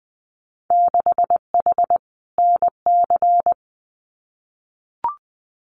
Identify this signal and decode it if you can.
Morse code, heard as a keyed tone, about 20 wpm, 700 Hz. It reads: 6H NC
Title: Winter Field Day Exchanges Encore 20wpm